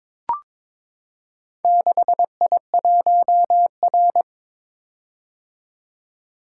Morse code, heard as a keyed tone, about 22 wpm, 700 Hz. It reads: 6I1R
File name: practice / 22wpm / Four Letter-Number Combinations - 22wpm